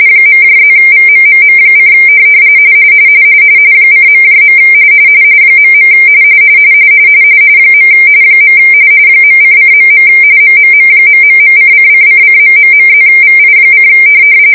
RTTY, about 45.45 baud. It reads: .+VMMM: :.. .. . .::.. MMM...:::::...+VI: